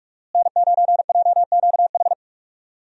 Morse code, heard as a keyed tone, about 46 wpm, 700 Hz. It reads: N9JQH
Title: USA Call Signs - 5 Characters - Repeated 2x - 45wpm